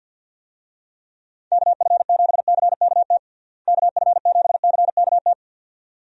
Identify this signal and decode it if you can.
Morse code, heard as a keyed tone, about 49 wpm, 700 Hz. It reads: KF7CKT KF7CKT
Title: USA Call Signs - 6 Characters - Repeated 2x - 50wpm